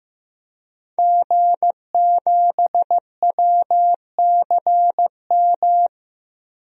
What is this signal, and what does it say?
Morse code, heard as a keyed tone, about 15 wpm, 700 Hz. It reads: G7WCM
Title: DXCC-100 Call Signs - 5 Characters - 15wpm